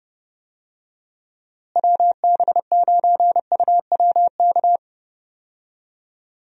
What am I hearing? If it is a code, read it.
Morse code, heard as a keyed tone, about 30 wpm, 700 Hz. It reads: WB9UWK
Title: USA Call Signs - 6 Characters - Repeated 2x - 30wpm